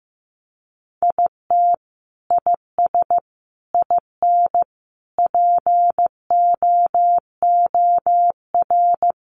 Morse code, heard as a keyed tone, about 15 wpm, 700 Hz. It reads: IT IS IN POOR
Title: Sentences from Top 1200 Words 15wpm